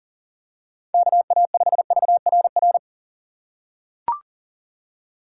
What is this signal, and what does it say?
Morse code, heard as a keyed tone, about 40 wpm, 700 Hz. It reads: KA5VRR
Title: USA Call Signs - 6 Characters - 3x Character Spacing - 40wpm